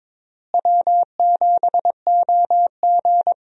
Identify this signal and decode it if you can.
Morse code, heard as a keyed tone, about 22 wpm, 700 Hz. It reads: W7OG
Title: USA Call Signs - 4 Characters - Repeated 3x - 22wpm